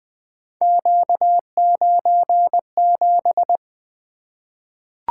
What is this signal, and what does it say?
Morse code, heard as a keyed tone, about 20 wpm, 700 Hz. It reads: Q97
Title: Three Letter-Number Combinations - 20wpm